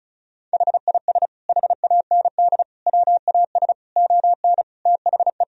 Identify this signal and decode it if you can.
Morse code, heard as a keyed tone, about 36 wpm, 700 Hz. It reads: HIS HAND WAS ON THE
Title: Sentences from Top 1000 Words - Review Part 1 - 35wpm